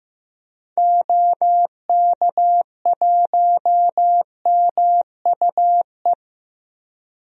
Morse code, heard as a keyed tone, about 15 wpm, 700 Hz. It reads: OK1MUE
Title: DXCC-100 Call Signs - 6 Characters - 15wpm